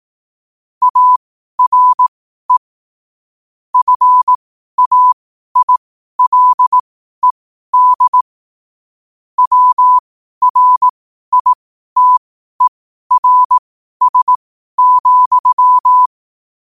Morse code, heard as a keyed tone, about 18 wpm, spaced as for 12 wpm, 1000 Hz. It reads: ARE FAILED WRITERS,
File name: qotdfiles / 12wpm